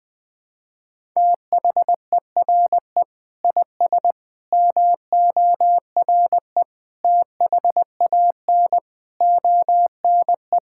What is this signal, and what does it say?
Morse code, heard as a keyed tone, about 21 wpm, 700 Hz. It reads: THERE IS MORE THAN ONE
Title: Sentences with 7-Word Phrases - Spoken Beginning - 20wpm